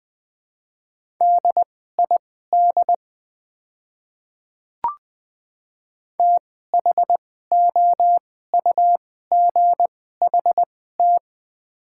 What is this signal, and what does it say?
Morse code, heard as a keyed tone, about 20 wpm, spaced as for 14 wpm, 700 Hz. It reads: DID THOUGHT
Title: 2x Character Spacing - Top 300 Words - 20wpm